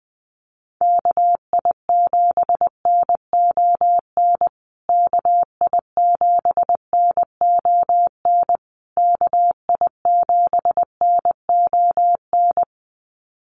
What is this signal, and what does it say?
Morse code, heard as a keyed tone, about 20 wpm, 700 Hz. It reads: KI7NON KI7NON KI7NON
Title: USA Call Signs Repeated 3x 20wpm